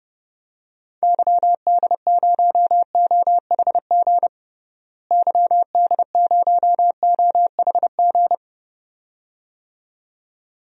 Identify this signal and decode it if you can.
Morse code, heard as a keyed tone, about 30 wpm, 700 Hz. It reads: YD0OHG YD0OHG
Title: DXCC-100 Call Signs - 6 Characters - Repeated 2x - 30wpm